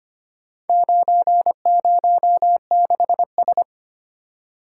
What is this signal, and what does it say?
Morse code, heard as a keyed tone, about 25 wpm, 700 Hz. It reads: 906S
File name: practice / 25wpm / Four Letter-Number Combinations - 25wpm